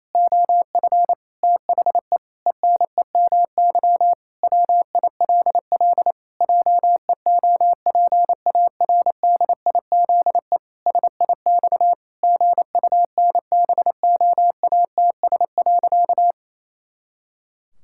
Morse code, heard as a keyed tone, about 27 wpm, 700 Hz. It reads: OF THE ENEMY WILL JEOPARDIZE SIX GUNBOATS.
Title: Pangram Challenge v2 - 28wpm